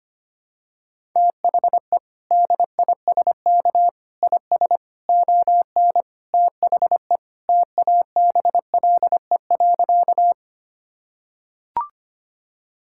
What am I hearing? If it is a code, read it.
Morse code, heard as a keyed tone, about 25 wpm, 700 Hz. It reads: THE DISK IS ON THE TABLE.
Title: Sentences from Top 1000 Words - Review Part 6 - 25wpm